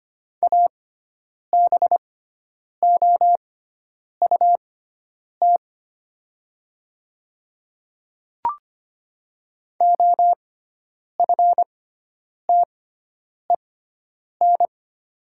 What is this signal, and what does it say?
Morse code, heard as a keyed tone, about 25 wpm, spaced as for 9 wpm, 700 Hz. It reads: ABOUT OFTEN
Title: 6x Character Spacing - Top 300 Words - 25wpm